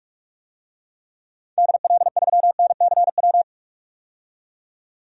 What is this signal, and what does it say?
Morse code, heard as a keyed tone, about 46 wpm, 700 Hz. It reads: DL3NCW
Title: DXCC-100 Call Signs - 6 Characters - 45wpm